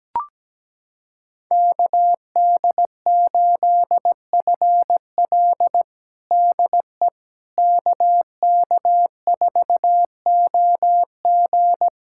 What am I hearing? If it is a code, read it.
Morse code, heard as a keyed tone, about 17 wpm, 700 Hz. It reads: KD8FL DE KK4OG
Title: USA Callsign DE Callsign - 5 Characters - 17wpm